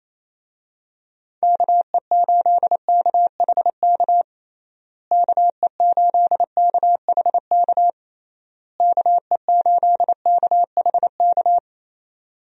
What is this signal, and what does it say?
Morse code, heard as a keyed tone, about 28 wpm, 700 Hz. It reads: KE8KHK KE8KHK KE8KHK
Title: USA Call Signs - 6 Characters - Repeated 3x - 28wpm